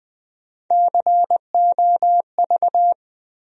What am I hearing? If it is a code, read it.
Morse code, heard as a keyed tone, about 20 wpm, 700 Hz. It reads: COV
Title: USA Call Sign Postfixes - 3 Letters 20wpm